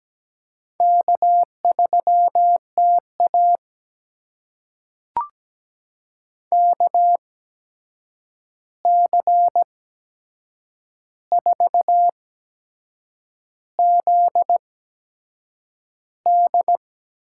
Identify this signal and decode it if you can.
Morse code, heard as a keyed tone, about 17 wpm, spaced as for 7 wpm, 700 Hz. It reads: K3TA KC4ZD
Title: USA Call Signs - 8x Character Spacing - 17wpm